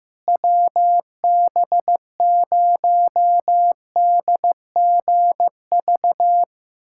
Morse code, heard as a keyed tone, about 15 wpm, 700 Hz. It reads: WB0DGV
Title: USA Callsign DE Callsign 15wpm